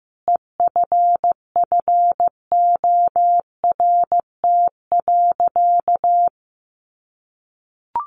Morse code, heard as a keyed tone, about 15 wpm, 700 Hz. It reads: EFFORT.
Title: Sentences from Top 1200 Words 15wpm